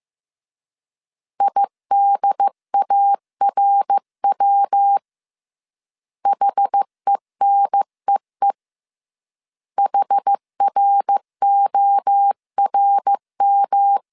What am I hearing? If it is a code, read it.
Morse code, heard as a keyed tone, about 15 wpm, 800 Hz. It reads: IDARW HENEE HRORM
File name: CW / DET / 1 3 1500C S A 14wpm 2447